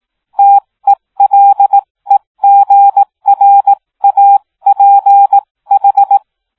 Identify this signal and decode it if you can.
Morse code, heard as a keyed tone, about 18 wpm, spaced as for 15 wpm, 800 Hz. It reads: TELEGRAPH